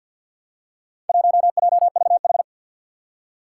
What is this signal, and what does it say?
Morse code, heard as a keyed tone, about 47 wpm, 700 Hz. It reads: 1JVH
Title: Four Challenging Letter-Number Combinations 50wpm